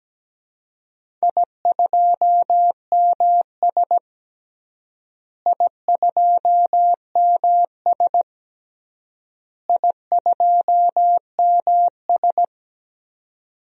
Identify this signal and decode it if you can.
Morse code, heard as a keyed tone, about 17 wpm, 700 Hz. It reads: I2MS I2MS I2MS
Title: DXCC-100 Call Signs - 4 Characters - Repeated 3x - 17wpm